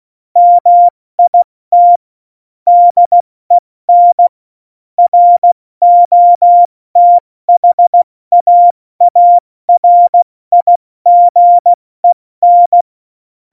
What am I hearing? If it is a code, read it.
Morse code, heard as a keyed tone, about 16 wpm, 700 Hz. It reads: MIT DEN ROTHAARIGEN
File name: Morsekurs / Witze / Witz 04 Juli 17, Tempo70 F80